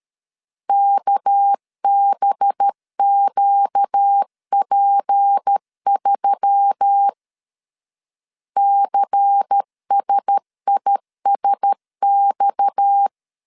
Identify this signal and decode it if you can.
Morse code, heard as a keyed tone, about 13 wpm, 800 Hz. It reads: KBQP3 CSISX
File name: CW / DET / 1 6 1500C S B 12wpm 3431